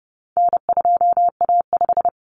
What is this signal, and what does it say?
Morse code, heard as a keyed tone, about 30 wpm, 700 Hz. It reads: N2A5
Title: Common Cut Numbers - 4-Digit 30wpm